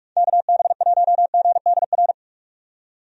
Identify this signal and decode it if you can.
Morse code, heard as a keyed tone, about 46 wpm, 700 Hz. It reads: KB1GDR
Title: USA Call Signs - 6 Characters - 45wpm